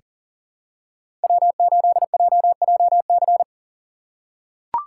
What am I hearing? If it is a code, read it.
Morse code, heard as a keyed tone, about 40 wpm, 700 Hz. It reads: W8JJC
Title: USA Call Signs - 5 Characters - 2x Character Spacing - 40wpm